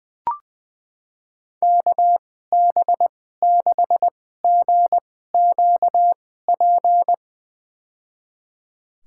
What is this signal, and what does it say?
Morse code, heard as a keyed tone, about 20 wpm, spaced as for 14 wpm, 700 Hz. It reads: KB6GQP
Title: USA Call Signs - 6 Characters - 2x Character Spacing - 20wpm